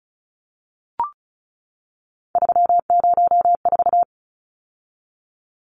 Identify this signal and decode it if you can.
Morse code, heard as a keyed tone, about 34 wpm, 700 Hz. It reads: 304
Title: Numbers - 3-Digit 35wpm